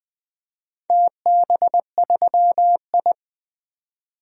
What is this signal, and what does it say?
Morse code, heard as a keyed tone, about 20 wpm, 700 Hz. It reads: TB3I
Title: DXCC-100 Call Signs - 4 Characters - 20wpm